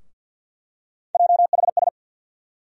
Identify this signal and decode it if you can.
Morse code, heard as a keyed tone, about 47 wpm, 700 Hz. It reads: JHS
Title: USA Call Sign Postfixes - 3 Letters 50wpm